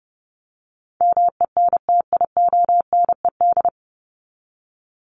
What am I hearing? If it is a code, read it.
Morse code, heard as a keyed tone, about 30 wpm, 700 Hz. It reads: MENTIONED
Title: ED Suffix - Top 100 Words with Base Words 30wpm